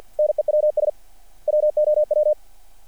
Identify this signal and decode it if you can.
Morse code, heard as a keyed tone, about 46 wpm, 600 Hz. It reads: NEWS WOW
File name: archived / CRSnet